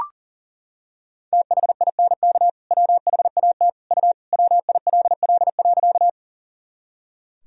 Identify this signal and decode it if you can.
Morse code, heard as a keyed tone, about 40 wpm, 700 Hz. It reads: THINK WHAT U WILL.
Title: Sentences from Top 200 Words and Abbreviations 40wpm